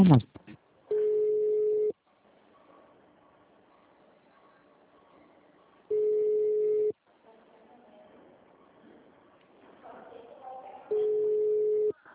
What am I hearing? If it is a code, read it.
Ringback tone (Europe, 425 Hz), with speech.
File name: out- -2004- - - .72